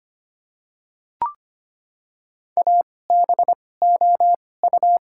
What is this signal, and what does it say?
Morse code, heard as a keyed tone, about 25 wpm, spaced as for 18 wpm, 690 Hz. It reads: ABOU
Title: 2x Character Spacing - Top 100 Words - 25wpm